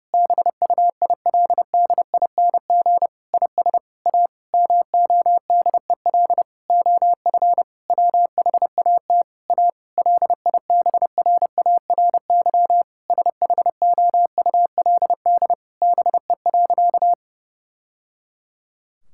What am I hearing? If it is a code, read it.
Morse code, heard as a keyed tone, about 32 wpm, 700 Hz. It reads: BUILDING IS A MODEL OF WHAT A LIBRARY SHOULD BE.
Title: Sentences from Top 1000 Words - Review Part 5 - 30wpm